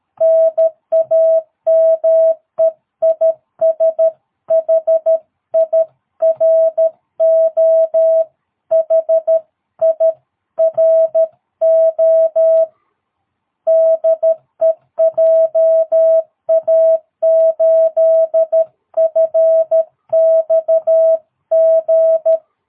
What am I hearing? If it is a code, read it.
Morse code, heard as a keyed tone, about 13 wpm, 660 Hz. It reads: NAME IS HIRO HIRO DE JA8FXG
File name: jh8cbh / 2022cw